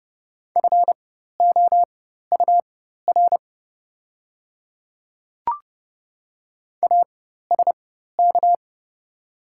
Morse code, heard as a keyed tone, about 31 wpm, spaced as for 14 wpm, 700 Hz. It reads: FOUR ASK
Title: 4x Character Spacing - Top 300 Words - 30wpm